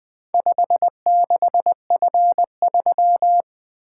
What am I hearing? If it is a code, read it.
Morse code, heard as a keyed tone, about 20 wpm, 700 Hz. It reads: 56F3
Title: Four Letter-Number Combinations - 20wpm